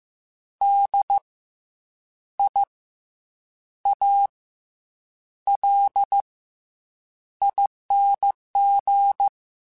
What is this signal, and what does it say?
Morse code, heard as a keyed tone, about 15 wpm, 800 Hz. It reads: D I A L ING
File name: Special INGs / Special INGs-mp3s / SlowToFaster ing @13